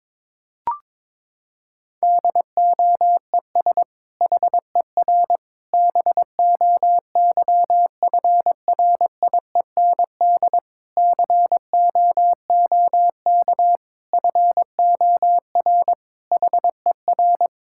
Morse code, heard as a keyed tone, about 22 wpm, 700 Hz. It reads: DOES HER BOYFRIEND COOK FOR HER
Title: Sentences from Top 1000 Words - Review Part 5 - 22wpm